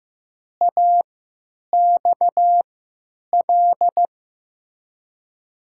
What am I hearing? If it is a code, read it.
Morse code, heard as a keyed tone, about 15 wpm, spaced as for 8 wpm, 700 Hz. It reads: AXL
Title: USA Call Signs - 3x Character Spacing - 15wpm